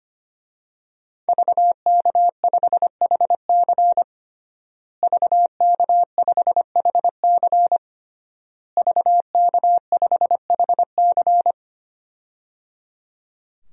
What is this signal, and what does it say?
Morse code, heard as a keyed tone, about 25 wpm, 700 Hz. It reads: VK5HC VK5HC VK5HC
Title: DXCC-100 Call Signs - 5 Characters - Repeated 3x - 25wpm